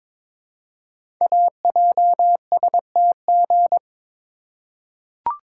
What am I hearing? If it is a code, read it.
Morse code, heard as a keyed tone, about 23 wpm, 700 Hz. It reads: AJSTG
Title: Code Groups - 5 Letters - 22wpm